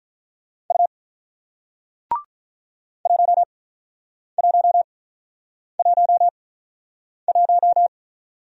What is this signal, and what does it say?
Morse code, heard as a keyed tone, about 47 wpm, spaced as for 9 wpm, 700 Hz. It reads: U 1111